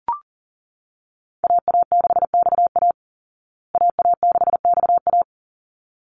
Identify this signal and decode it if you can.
Morse code, heard as a keyed tone, about 40 wpm, 700 Hz. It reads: AA6XA AA6XA
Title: Call Signs Repeated 3x - 40wpm